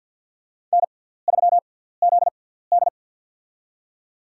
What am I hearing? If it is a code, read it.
Morse code, heard as a keyed tone, about 47 wpm, spaced as for 17 wpm, 700 Hz. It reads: N3ZD
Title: USA Call Signs - 4 Characters - 6x Character Spacing - 50wpm